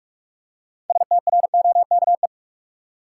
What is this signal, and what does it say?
Morse code, heard as a keyed tone, about 47 wpm, 700 Hz. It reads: STROKE
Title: Six Letter Words - 45wpm